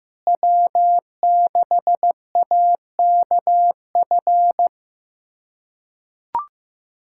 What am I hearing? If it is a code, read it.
Morse code, heard as a keyed tone, about 15 wpm, 700 Hz. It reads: W6AKF
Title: USA Callsign DE Callsign - 5 Characters - 15wpm